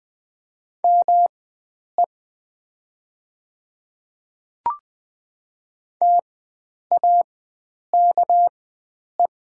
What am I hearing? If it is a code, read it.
Morse code, heard as a keyed tone, about 20 wpm, spaced as for 9 wpm, 700 Hz. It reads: ME TAKE
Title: 4x Character Spacing - Top 500 Words - 20wpm